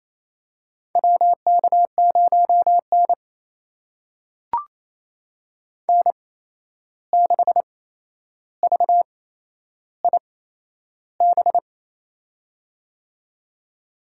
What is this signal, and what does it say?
Morse code, heard as a keyed tone, about 28 wpm, spaced as for 11 wpm, 700 Hz. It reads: WK0N N6VIB
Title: USA Call Signs - 8x Character Spacing - 28wpm